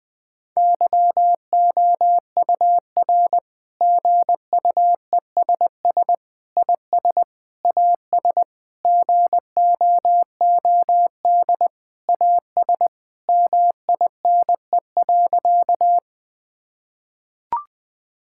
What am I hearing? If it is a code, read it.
Morse code, heard as a keyed tone, about 20 wpm, 700 Hz. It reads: YOUR GUESS IS AS GOOD AS MINE.
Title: Sentences with 7-Word Phrases - 20wpm